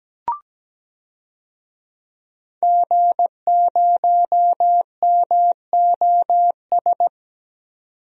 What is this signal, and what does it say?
Morse code, heard as a keyed tone, about 17 wpm, 700 Hz. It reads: G0MOS
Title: DXCC-100 Call Signs - 5 Characters - 17wpm